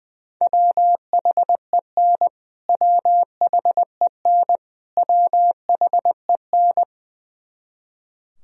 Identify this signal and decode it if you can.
Morse code, heard as a keyed tone, about 20 wpm, 700 Hz. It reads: WHEN WHEN WHEN
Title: Top 100 Words Repeated 20wpm